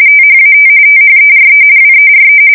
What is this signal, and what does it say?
RTTY, about 45.45 baud. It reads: THAT THERE WERE
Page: 170 Shift 2125/2295